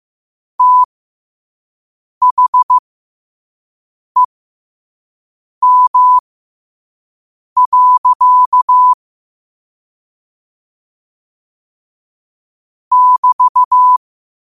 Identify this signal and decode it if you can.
Morse code, heard as a keyed tone, about 15 wpm, spaced as for 5 wpm, 1000 Hz. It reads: THEM. <BT>
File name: qotdfiles / 5wpm